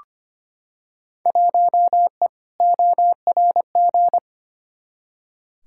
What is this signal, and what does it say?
Morse code, heard as a keyed tone, about 25 wpm, 700 Hz. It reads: 1E ORG
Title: Field Day Exchanges - 25wpm